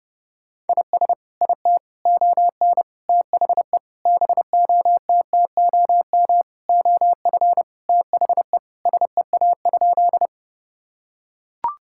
Morse code, heard as a keyed tone, about 30 wpm, 700 Hz. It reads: IS IT ON THE BOTTOM OF THE SEA?
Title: Sentences from Top 1000 Words - Review Part 6 - 30wpm